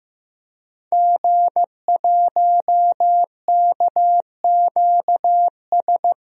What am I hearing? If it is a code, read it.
Morse code, heard as a keyed tone, about 15 wpm, 700 Hz. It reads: G1KQS
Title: DXCC-100 Call Signs - 5 Characters - Repeated 2x - 15wpm